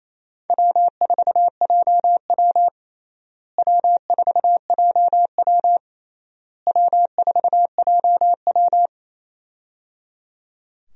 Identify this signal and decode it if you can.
Morse code, heard as a keyed tone, about 28 wpm, 700 Hz. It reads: W4JW W4JW W4JW
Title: USA Call Signs - 4 Characters - Repeated 3x - 28wpm